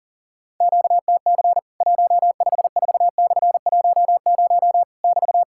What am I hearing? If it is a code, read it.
Morse code, heard as a keyed tone, about 40 wpm, 700 Hz. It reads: QTC 154/10 <BT>
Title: Worked All Europe DX Contest - QTC Exchange - 40wpm